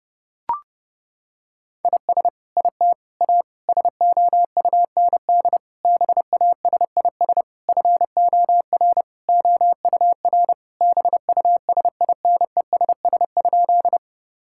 Morse code, heard as a keyed tone, about 30 wpm, 700 Hz. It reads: IS IT A SOUND BASIS FOR OUR BUSINESS?
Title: Sentences from Top 1000 Words - Review Part 5 - 30wpm